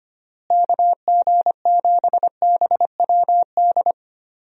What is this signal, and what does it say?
Morse code, heard as a keyed tone, about 25 wpm, 700 Hz. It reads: KG7BWD
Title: USA Call Signs - 6 Characters - Repeated 3x - 25wpm